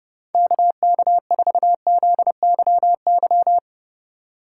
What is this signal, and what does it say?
Morse code, heard as a keyed tone, about 30 wpm, 700 Hz. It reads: KK4ZYY